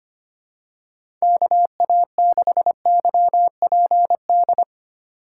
Morse code, heard as a keyed tone, about 25 wpm, 690 Hz. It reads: KA6YPD
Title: USA Call Signs - 6 Characters - 3x Character Spacing - 25wpm